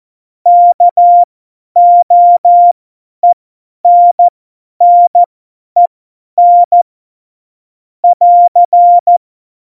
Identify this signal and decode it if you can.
Morse code, heard as a keyed tone, about 14 wpm, spaced as for 10 wpm, 700 Hz. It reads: KOENNEN <AR>
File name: Morsekurs / Witze / Witz 11. Dezember, 50 F70